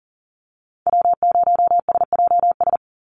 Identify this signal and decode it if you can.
Morse code, heard as a keyed tone, about 40 wpm, 700 Hz. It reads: W0SJS
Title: Call Signs Repeated 2x - 40wpm